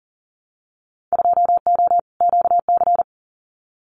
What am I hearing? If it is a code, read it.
Morse code, heard as a keyed tone, about 40 wpm, 700 Hz. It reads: 2O QC
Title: Winter Field Day Exchanges 40wpm